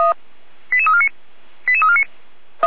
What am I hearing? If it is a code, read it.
Five-tone selective call (ZVEI), heard as quick runs of five tones, tones of about 80 ms. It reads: 88349 88349
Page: /455/SDR/ZVEI/88243/2022.10.29/